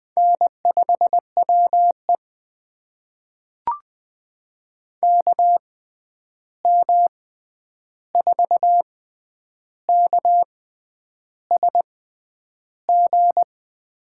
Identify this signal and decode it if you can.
Morse code, heard as a keyed tone, about 20 wpm, spaced as for 9 wpm, 700 Hz. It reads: N5WE KM4KSG
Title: USA Call Signs - 6x Character Spacing - 20wpm